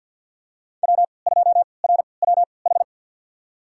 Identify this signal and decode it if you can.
Morse code, heard as a keyed tone, about 53 wpm, spaced as for 29 wpm, 700 Hz. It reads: W2RWH
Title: USA Call Signs - 5 Characters - 3x Character Spacing - 50wpm